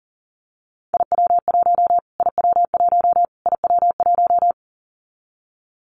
Morse code, heard as a keyed tone, about 40 wpm, 700 Hz. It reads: IW1 IW1 IW1
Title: Callsign Prefixes for Top 25 DXCC - Repeated 3x - 40wpm